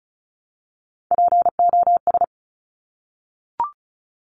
Morse code, heard as a keyed tone, about 36 wpm, 700 Hz. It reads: POS
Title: Top Three Letter Combinations in Words 35wpm